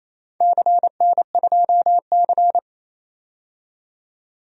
Morse code, heard as a keyed tone, about 28 wpm, 700 Hz. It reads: CN2C
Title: DXCC-100 Call Signs - 4 Characters - 28wpm